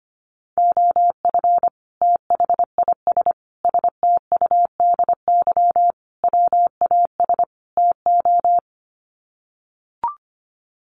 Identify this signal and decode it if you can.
Morse code, heard as a keyed tone, about 25 wpm, 700 Hz. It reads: OF THIS STUDY WAS TO
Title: Sets of 5 Words 25wpm